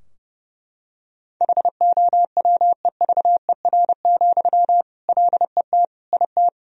Morse code, heard as a keyed tone, about 30 wpm, 700 Hz. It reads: HOWEVER, LET IT
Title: Sentences from Top 1000 Words - Review Part 2 - 30wpm